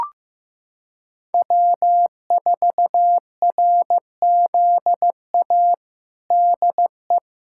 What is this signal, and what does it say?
Morse code, heard as a keyed tone, about 15 wpm, 700 Hz. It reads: W4RZA DE
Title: Practice QSO - Exchange 4 - Ending - 15wpm